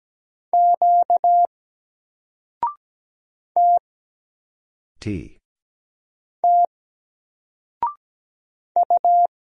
Morse code, heard as a keyed tone, about 17 wpm, 700 Hz. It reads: Q T T U
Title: Single Letter-Number 17wpm